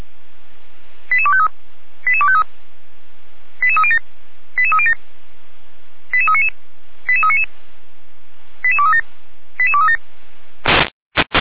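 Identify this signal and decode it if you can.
Five-tone selective call (ZVEI), heard as quick runs of five tones, tones of about 70 ms. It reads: 88253 88253 88307 88307 88399 88399 88237 88237
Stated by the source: Five-tone ZVEI selective call received by SDR, from audio.f85360.feuerwehr-orlamuende.org fire brigade server.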